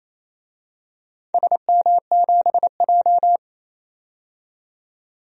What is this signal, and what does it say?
Morse code, heard as a keyed tone, about 28 wpm, 700 Hz. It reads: SM7J
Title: DXCC-100 Call Signs - 4 Characters - 28wpm